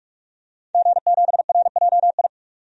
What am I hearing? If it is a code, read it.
Morse code, heard as a keyed tone, about 46 wpm, 700 Hz. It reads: G7RJI
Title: DXCC-100 Call Signs - 5 Characters - 45wpm